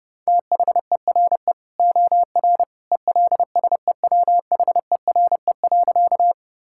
Morse code, heard as a keyed tone, about 30 wpm, 700 Hz. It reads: THERE OR ELSEWHERE.
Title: Sentences from Top 1600 Words 30wpm